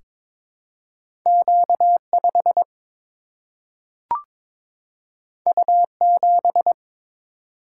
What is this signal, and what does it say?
Morse code, heard as a keyed tone, about 22 wpm, 700 Hz. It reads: Q 5 U 7
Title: Single-Letter Single-Number 22wpm